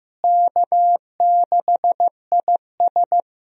Morse code, heard as a keyed tone, about 15 wpm, 700 Hz. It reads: K6IS
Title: USA Call Signs - 4 Characters - Repeated 3x - 15wpm